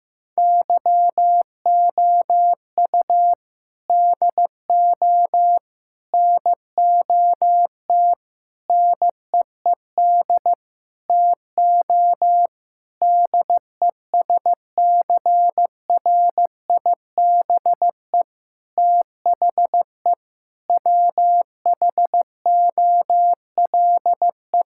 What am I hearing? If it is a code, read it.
Morse code, heard as a keyed tone, about 15 wpm, 700 Hz. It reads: YOU DO NOT NEED TO DESCRIBE THE WHOLE
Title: Sentences from Top 1600 Words 15wpm